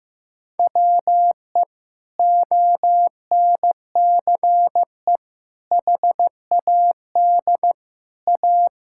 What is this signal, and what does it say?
Morse code, heard as a keyed tone, about 15 wpm, 700 Hz. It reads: WE ONCE HAD A
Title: Sentences from Top 1000 Words - Review Part 6 - 15wpm